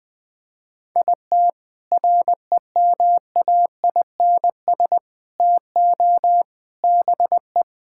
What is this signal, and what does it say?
Morse code, heard as a keyed tone, about 20 wpm, 700 Hz. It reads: IT REMAINS TO BE
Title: Sets of 6 Words - 20wpm